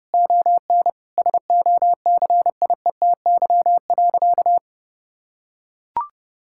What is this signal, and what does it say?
Morse code, heard as a keyed tone, about 30 wpm, 700 Hz. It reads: ON SOCIETY.
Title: Sentences from Top 1000 Words - Review Part 6 - 30wpm